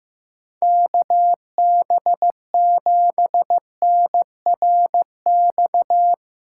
Morse code, heard as a keyed tone, about 15 wpm, 700 Hz. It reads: KB7NRX
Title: USA Callsign DE Callsign 15wpm